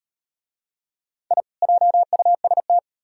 Morse code, heard as a keyed tone, about 38 wpm, 700 Hz. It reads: I JUST